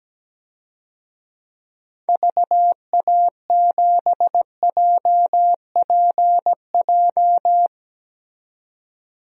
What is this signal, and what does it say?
Morse code, heard as a keyed tone, about 17 wpm, 700 Hz. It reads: VA7JPJ
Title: DXCC-100 Call Signs - 6 Characters - 17wpm